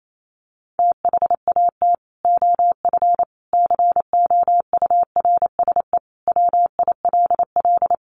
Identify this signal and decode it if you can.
Morse code, heard as a keyed tone, about 28 wpm, 700 Hz. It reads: THAT OF COURSE WILL
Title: Sentences from Top 1500 Words 28wpm